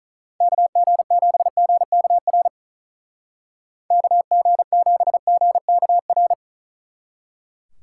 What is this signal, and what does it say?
Morse code, heard as a keyed tone, about 46 wpm, 700 Hz. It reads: KG7GKR KG7GKR